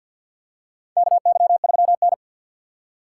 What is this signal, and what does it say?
Morse code, heard as a keyed tone, about 47 wpm, 700 Hz. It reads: KY3N
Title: USA Call Signs - 4 Characters - 6x Character Spacing - 50wpm